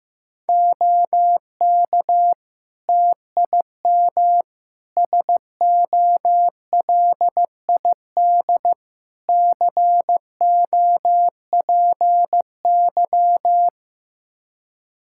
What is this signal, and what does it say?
Morse code, heard as a keyed tone, about 15 wpm, 700 Hz. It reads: OK TIM SOLID COPY
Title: Practice QSO - Exchange 3 - More Info - 15wpm